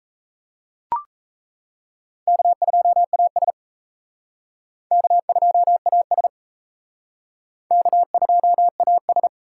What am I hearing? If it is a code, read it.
Morse code, heard as a keyed tone, about 36 wpm, 700 Hz. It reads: K2AS K2AS K2AS